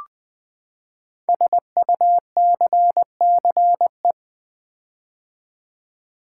Morse code, heard as a keyed tone, about 20 wpm, 700 Hz. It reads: SUCCE
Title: Build A Word - 5 Letters At A Time - 10-Letter Words - 20wpm